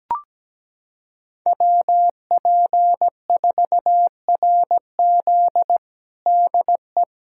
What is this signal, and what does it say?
Morse code, heard as a keyed tone, about 17 wpm, 700 Hz. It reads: WP4RZ DE
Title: USA Callsign DE Callsign - 5 Characters - 17wpm